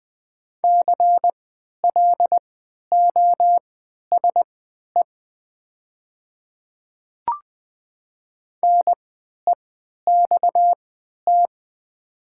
Morse code, heard as a keyed tone, about 20 wpm, spaced as for 11 wpm, 700 Hz. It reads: CLOSE NEXT
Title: 3x Character Spacing - Top 300 Words - 20wpm